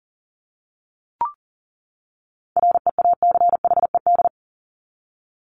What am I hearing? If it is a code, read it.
Morse code, heard as a keyed tone, about 40 wpm, 700 Hz. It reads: REACHED
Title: ED Suffix - Top 100 Words with Base Words 40wpm